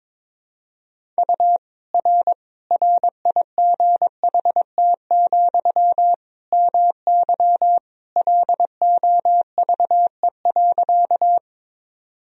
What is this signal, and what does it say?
Morse code, heard as a keyed tone, about 22 wpm, 700 Hz. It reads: U R RIGHT, MY LOVE.
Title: Sentences from Top 400 Words and Abbreviations 22wpm